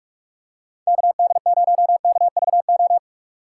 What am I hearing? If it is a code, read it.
Morse code, heard as a keyed tone, about 42 wpm, 700 Hz. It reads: KD0KVO
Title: USA Call Signs - 2x Character Spacing - 45wpm